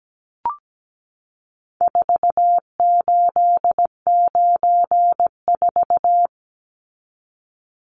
Morse code, heard as a keyed tone, about 17 wpm, 700 Hz. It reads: 4894
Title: Numbers - 4-Digit 17wpm